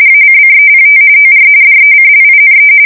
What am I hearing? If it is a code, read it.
RTTY, about 45.45 baud. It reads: ACCEPT OR DECLINE
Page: 170 Shift 2125/2295